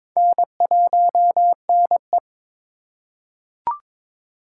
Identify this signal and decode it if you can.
Morse code, heard as a keyed tone, about 22 wpm, 700 Hz. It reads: N1NE
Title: USA Call Signs - 2x Character Spacing - 22wpm